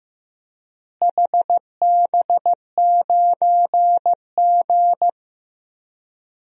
Morse code, heard as a keyed tone, about 15 wpm, 700 Hz. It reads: HB9G
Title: DXCC-100 Call Signs - 4 Characters - Repeated 2x - 15wpm